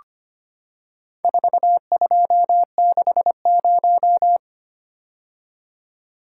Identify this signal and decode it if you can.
Morse code, heard as a keyed tone, about 25 wpm, 700 Hz. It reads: 4260
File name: practice / 25wpm / Four Letter-Number Combinations - 25wpm